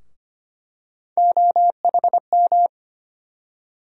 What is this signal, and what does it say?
Morse code, heard as a keyed tone, about 25 wpm, 700 Hz. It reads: OHM
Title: USA Call Sign Postfixes - 3 Letters 25wpm